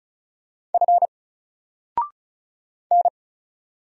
Morse code, heard as a keyed tone, about 34 wpm, 700 Hz. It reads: F N
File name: Single-Letter-Number-35wpm